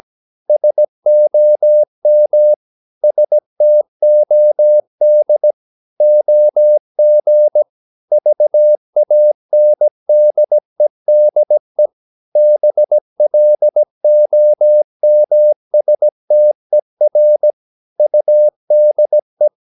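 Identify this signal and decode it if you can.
Morse code, heard as a keyed tone, about 17 wpm, 600 Hz. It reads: SOM STOD OG VANDEDE BLOMSTER UDE
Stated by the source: Slow (QRS) Morse practice at 17 wpm, from CW med Gnister.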